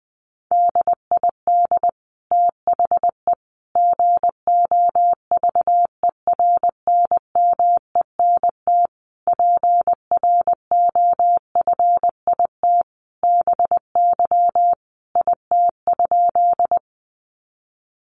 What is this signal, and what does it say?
Morse code, heard as a keyed tone, about 20 wpm, 700 Hz. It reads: DID THE GOVERNMENT PROFIT BY IT?
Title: Sentences from Top 800 Words 20wpm